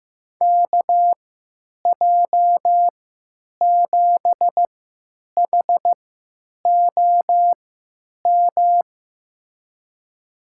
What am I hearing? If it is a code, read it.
Morse code, heard as a keyed tone, about 15 wpm, spaced as for 9 wpm, 700 Hz. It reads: KJ7HOM
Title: USA Call Signs - 6 Characters - 3x Character Spacing - 15wpm